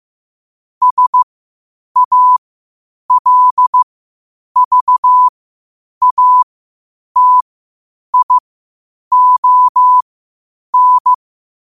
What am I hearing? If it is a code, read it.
Morse code, heard as a keyed tone, about 15 wpm, spaced as for 8 wpm, 1000 Hz. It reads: SALVATION
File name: qotdfiles / 8wpm